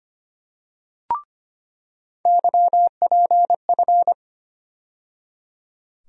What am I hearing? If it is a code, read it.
Morse code, heard as a keyed tone, about 25 wpm, 700 Hz. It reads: YPF
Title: USA Call Sign Postfixes - 3 Letters 25wpm